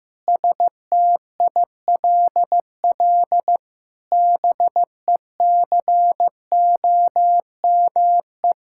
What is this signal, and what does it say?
Morse code, heard as a keyed tone, about 15 wpm, 700 Hz. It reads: STILL BECOME
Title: Sentences from Top 1000 Words - Review Part 6 - 15wpm